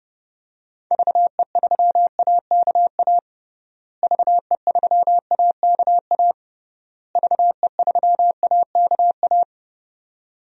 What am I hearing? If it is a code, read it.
Morse code, heard as a keyed tone, about 30 wpm, 700 Hz. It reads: VE3AKA VE3AKA VE3AKA
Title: DXCC-100 Call Signs - 6 Characters - Repeated 3x - 30wpm